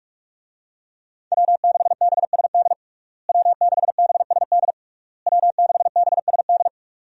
Morse code, heard as a keyed tone, about 41 wpm, 700 Hz. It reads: W6BSD W6BSD W6BSD
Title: USA Call Signs - 5 Characters - Repeated 3x - 45wpm